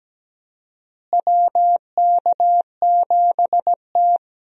Morse code, heard as a keyed tone, about 17 wpm, 700 Hz. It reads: WK7T
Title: USA Call Signs - 4 Characters - Repeated 2x - 17wpm